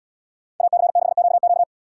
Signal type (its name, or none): Morse code